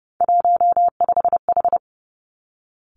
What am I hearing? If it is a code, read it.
Morse code, heard as a keyed tone, about 30 wpm, 700 Hz. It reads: 15H
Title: Three Challenging Letter-Number Combinations 30wpm